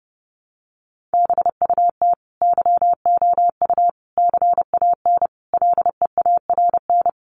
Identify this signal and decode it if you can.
Morse code, heard as a keyed tone, about 30 wpm, 700 Hz. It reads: BUT YOU CAN LEARN
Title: Sentences from Top 1500 Words 30wpm